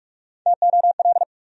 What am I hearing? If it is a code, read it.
Morse code, heard as a keyed tone, about 41 wpm, 700 Hz. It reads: TOL
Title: USA Call Sign Postfixes - 3 Letters 45wpm